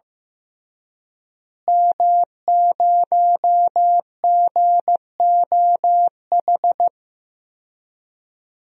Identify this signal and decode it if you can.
Morse code, heard as a keyed tone, about 15 wpm, 700 Hz. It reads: M0GOH
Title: DXCC-100 Call Signs - 5 Characters - Repeated 2x - 15wpm